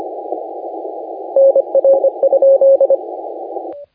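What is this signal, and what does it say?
Morse code, heard as a keyed tone, about 25 wpm, 560 Hz, with radio noise. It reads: NR?
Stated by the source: Morse contest audio recorded by LA4C during the SAC CW 2018 contest.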